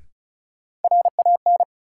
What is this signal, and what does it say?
Morse code, heard as a keyed tone, about 34 wpm, 700 Hz. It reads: RAN
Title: Sentences from Top 500 Words - New Word Drill - 35wpm